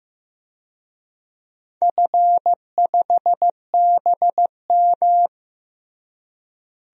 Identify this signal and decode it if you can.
Morse code, heard as a keyed tone, about 15 wpm, 700 Hz. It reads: F5BM
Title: DXCC-100 Call Signs - 4 Characters - Repeated 3x - 15wpm